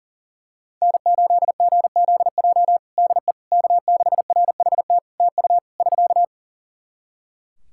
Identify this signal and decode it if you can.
Morse code, heard as a keyed tone, about 37 wpm, 700 Hz. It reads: N8GZJ DE K6RHT TU <SK>
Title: Practice QSO - Exchange 4 - Ending - 40wpm